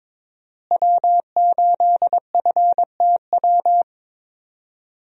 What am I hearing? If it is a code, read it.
Morse code, heard as a keyed tone, about 22 wpm, 700 Hz. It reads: W8FTW
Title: USA Call Signs - 5 Characters - Repeated 2x - 22wpm